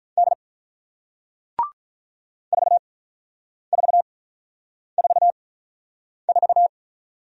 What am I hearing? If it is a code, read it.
Morse code, heard as a keyed tone, about 46 wpm, spaced as for 9 wpm, 700 Hz. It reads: D 4444